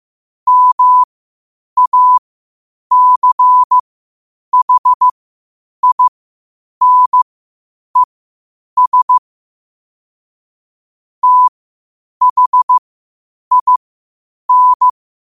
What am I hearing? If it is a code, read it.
Morse code, heard as a keyed tone, about 15 wpm, spaced as for 8 wpm, 1000 Hz. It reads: MACHINES THIN
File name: qotdfiles / 8wpm